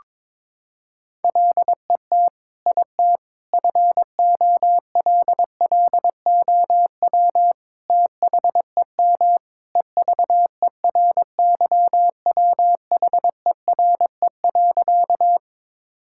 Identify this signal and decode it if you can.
Morse code, heard as a keyed tone, about 22 wpm, 700 Hz. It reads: LET IT FOLLOW THEM EVERYWHERE.
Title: Sentences from Top 1600 Words 22wpm